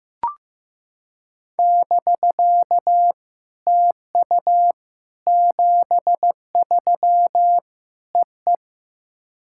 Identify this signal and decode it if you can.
Morse code, heard as a keyed tone, about 15 wpm, 700 Hz. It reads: <BK> TU 73 EE
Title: SOTA QSO Practice 15wpm